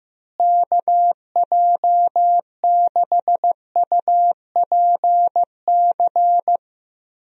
Morse code, heard as a keyed tone, about 15 wpm, 700 Hz. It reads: KJ6UPC
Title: USA Call Signs - 6 Characters - Repeated 2x - 15wpm